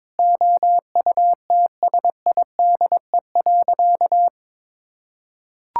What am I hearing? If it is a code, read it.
Morse code, heard as a keyed tone, about 22 wpm, 700 Hz. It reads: OUTSIDE.
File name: practice / 22wpm / Sentences with 3-Word Phrases - 22wpm